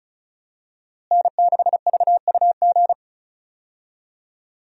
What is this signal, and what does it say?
Morse code, heard as a keyed tone, about 34 wpm, 700 Hz. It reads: N6VUG
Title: USA Call Signs - 5 Characters - Repeated 2x - 35wpm